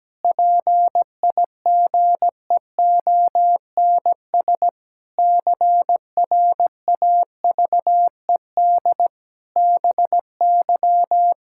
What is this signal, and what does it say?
Morse code, heard as a keyed tone, about 17 wpm, 700 Hz. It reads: PIGEONS CRAVED BY
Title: Pangram Challenge v2 - 17wpm